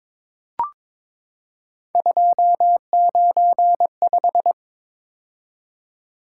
Morse code, heard as a keyed tone, about 22 wpm, 700 Hz. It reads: 295
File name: practice / 22wpm / Three Letter-Number Combinations - 22wpm